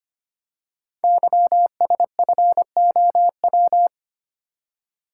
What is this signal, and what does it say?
Morse code, heard as a keyed tone, about 25 wpm, 700 Hz. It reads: YSFOW
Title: Code Groups - 5 Letter:Numbers - 25wpm